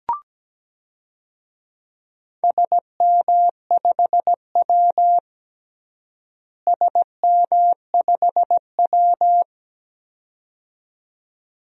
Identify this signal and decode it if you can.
Morse code, heard as a keyed tone, about 17 wpm, 700 Hz. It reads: SM5W SM5W
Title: DXCC-100 Call Signs - 4 Characters - Repeated 2x - 17wpm